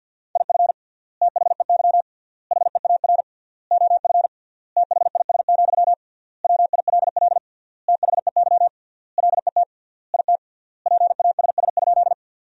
Morse code, heard as a keyed tone, about 50 wpm, 700 Hz. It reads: IF THEY HEAR OF THIS, WILL THEY LET IT PASS?
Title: 3x Word-Spacing - Sentences from Top 400 Words - 50wpm